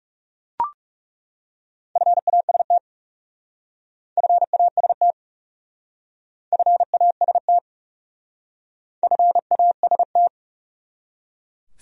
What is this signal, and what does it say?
Morse code, heard as a keyed tone, about 39 wpm, 700 Hz. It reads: FAST FAST FAST FAST